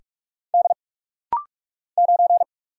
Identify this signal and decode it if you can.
Morse code, heard as a keyed tone, about 44 wpm, 700 Hz. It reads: D 9
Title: Single Letter-Number - Rapid-Fire 45wpm